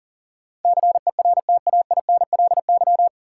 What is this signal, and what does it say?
Morse code, heard as a keyed tone, about 40 wpm, 700 Hz. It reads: CERTAINLY
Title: Sentences from Top 900 Words - New Word Drill - 40wpm